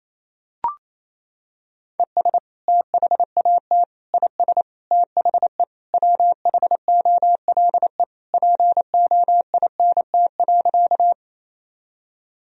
Morse code, heard as a keyed tone, about 27 wpm, 700 Hz. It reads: ES THAT IS THE WHOLE POINT.
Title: Sentences from Top 400 Words and Abbreviations 28wpm